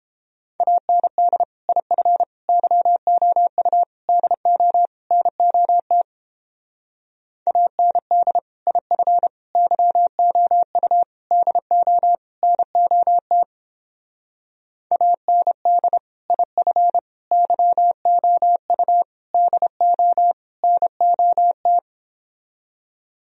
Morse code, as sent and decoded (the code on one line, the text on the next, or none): .- -. -.. / .. ..-. / -.-- --- ..- / -.. --- / -. --- - / .- -. -.. / .. ..-. / -.-- --- ..- / -.. --- / -. --- - / .- -. -.. / .. ..-. / -.-- --- ..- / -.. --- / -. --- -
AND IF YOU DO NOT AND IF YOU DO NOT AND IF YOU DO NOT